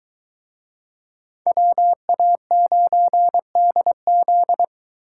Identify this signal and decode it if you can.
Morse code, heard as a keyed tone, about 21 wpm, 700 Hz. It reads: WA9DZ